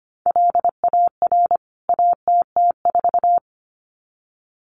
Morse code, heard as a keyed tone, about 25 wpm, 700 Hz. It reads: LAR ATT4
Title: CWT Exchange 25wpm